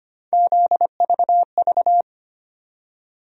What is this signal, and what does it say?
Morse code, heard as a keyed tone, about 25 wpm, 700 Hz. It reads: ZVV
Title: USA Call Sign Postfixes - 3 Letters 25wpm